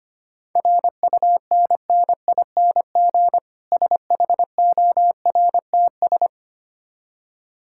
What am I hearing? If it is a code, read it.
Morse code, heard as a keyed tone, about 25 wpm, 700 Hz. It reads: RUNNING SHORTS
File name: practice / 25wpm / Difficult Words with Hints - 25wpm